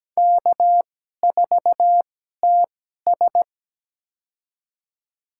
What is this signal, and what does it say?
Morse code, heard as a keyed tone, about 17 wpm, spaced as for 12 wpm, 700 Hz. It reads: K4TS
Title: USA Call Signs - 4 Characters - 2x Character Spacing - 17wpm